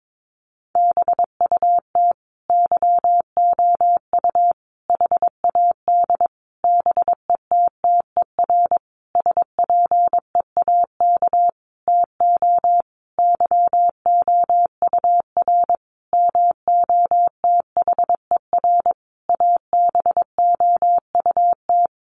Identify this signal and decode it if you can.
Morse code, heard as a keyed tone, about 22 wpm, 700 Hz. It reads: BUT YOU HAD BETTER SPEAK TO YOUR MOTHER ABOUT
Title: Sentences from Top 1100 Words 22wpm